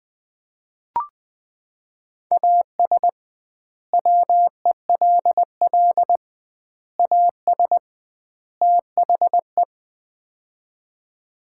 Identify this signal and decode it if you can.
Morse code, heard as a keyed tone, about 20 wpm, 700 Hz. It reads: AS WELL AS THE
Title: 2x Word-Spacing - Sets of 4 Words - 20wpm